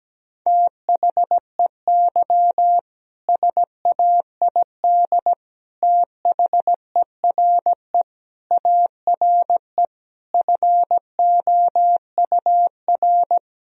Morse code, heard as a keyed tone, about 17 wpm, 700 Hz. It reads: THEY SAID THERE ARE FOUR
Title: Sentences from Top 1000 Words - Review Part 2 - 17wpm